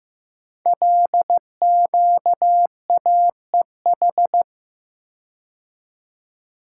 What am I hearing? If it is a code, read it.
Morse code, heard as a keyed tone, about 15 wpm, 700 Hz. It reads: LQAEH
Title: Code Groups - 5 Letters - 15wpm